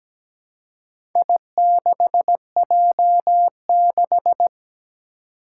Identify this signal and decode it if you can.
Morse code, heard as a keyed tone, about 17 wpm, 700 Hz. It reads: I6J6
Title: Four Challenging Letter-Number Combinations 17wpm